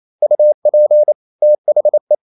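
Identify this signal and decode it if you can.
Morse code, heard as a keyed tone, about 28 wpm, 600 Hz. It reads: UP THE
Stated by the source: Morse code practice at 28 wpm, text of Ken Follett's novel Never in English, from CW med Gnister.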